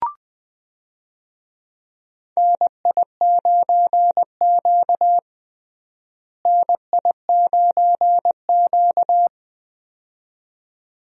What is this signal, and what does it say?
Morse code, heard as a keyed tone, about 21 wpm, 700 Hz. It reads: NI9Q NI9Q
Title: USA Call Signs - 4 Characters - Repeated 2x - 20wpm